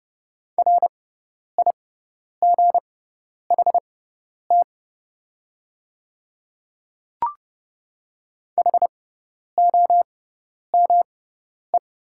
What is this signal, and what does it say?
Morse code, heard as a keyed tone, about 31 wpm, spaced as for 10 wpm, 700 Hz. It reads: RIGHT HOME
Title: 6x Character Spacing - Top 200 Words - 30wpm